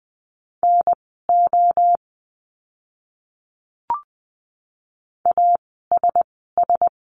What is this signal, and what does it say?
Morse code, heard as a keyed tone, about 20 wpm, spaced as for 14 wpm, 700 Hz. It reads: NO ASS
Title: 036 Lesson 7f - Letter s - Words 20wpm